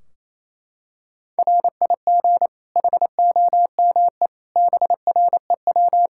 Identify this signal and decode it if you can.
Morse code, heard as a keyed tone, about 28 wpm, 700 Hz. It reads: RIG HOME BREW
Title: Practice QSO - Exchange 2 - Info - 28wpm